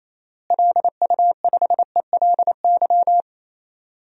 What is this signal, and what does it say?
Morse code, heard as a keyed tone, about 28 wpm, 700 Hz. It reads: LU5ELY
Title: DXCC-100 Call Signs - 6 Characters - Repeated 2x - 28wpm